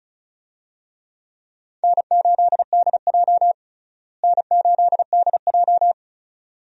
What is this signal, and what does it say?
Morse code, heard as a keyed tone, about 36 wpm, 700 Hz. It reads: N8DJ N8DJ
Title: USA Call Signs - 4 Characters - Repeated 2x - 35wpm